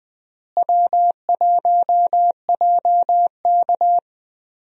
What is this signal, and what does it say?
Morse code, heard as a keyed tone, about 20 wpm, 700 Hz. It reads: W1JK
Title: USA Call Signs - 4 Characters - Repeated 3x - 20wpm